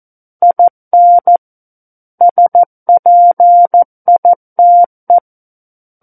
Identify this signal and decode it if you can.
Morse code, heard as a keyed tone, about 14 wpm, 700 Hz. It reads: IN SPITE
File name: Book One / 05-Chapter 5 - 14 WPM